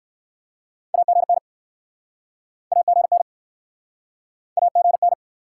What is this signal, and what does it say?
Morse code, heard as a keyed tone, about 63 wpm, 700 Hz. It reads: AGN AGN AGN